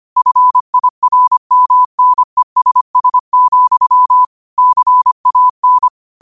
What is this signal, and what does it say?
Morse code, heard as a keyed tone, about 25 wpm, 1000 Hz. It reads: FIRMNESS, CAN
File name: qotdfiles / 25wpm